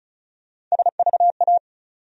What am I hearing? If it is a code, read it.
Morse code, heard as a keyed tone, about 36 wpm, 700 Hz. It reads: SVA
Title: USA Call Sign Postfixes - 3 Letters 35wpm